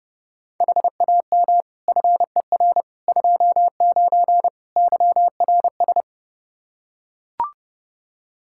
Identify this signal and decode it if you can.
Morse code, heard as a keyed tone, about 29 wpm, 700 Hz. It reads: HAM FER 29 YRS
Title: Practice QSO - Exchange 3 - More Info - 30wpm